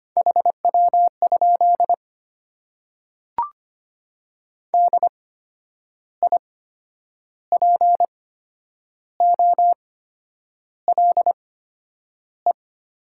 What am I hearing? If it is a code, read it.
Morse code, heard as a keyed tone, about 25 wpm, spaced as for 9 wpm, 700 Hz. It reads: HW? DIPOLE
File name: Top 90 QSO Elements - 8x Character Spacing - 25wpm